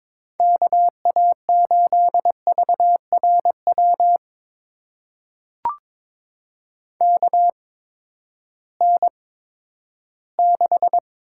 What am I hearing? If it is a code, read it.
Morse code, heard as a keyed tone, about 22 wpm, spaced as for 13 wpm, 700 Hz. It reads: KA8VRW KN6
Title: USA Call Signs - 6 Characters - 8x Character Spacing - 22wpm